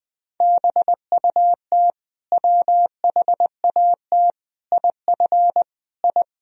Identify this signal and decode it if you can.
Morse code, heard as a keyed tone, about 20 wpm, 700 Hz. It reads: BUT WHAT IF I
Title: Sentences from Top 1000 Words - Review Part 2 - 20wpm